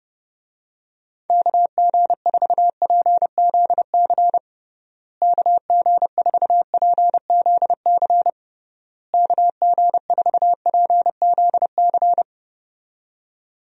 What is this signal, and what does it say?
Morse code, heard as a keyed tone, about 30 wpm, 700 Hz. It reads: KG4PZC KG4PZC KG4PZC
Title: USA Call Signs - 6 Characters - Repeated 3x - 30wpm